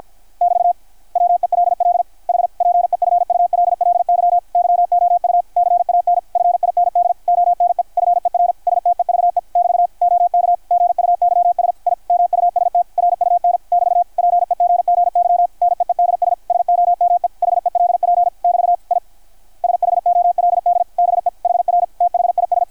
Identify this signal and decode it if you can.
Morse code, heard as a keyed tone, about 53 wpm, 660 Hz, with radio noise. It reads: <BT> WELL H PEFULLY YOU CAN FIND ONE FER STEVE <BT> OK GUYS I MUST RUN <BT> PEGGY NEEDS SOME HELP <BT> I SHOLD BE HR THIS
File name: archived / CRSnet